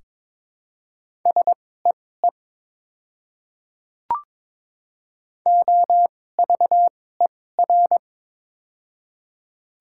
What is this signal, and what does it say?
Morse code, heard as a keyed tone, about 22 wpm, spaced as for 16 wpm, 700 Hz. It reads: SEE OVER
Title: 2x Character Spacing - Top 200 Words - 22wpm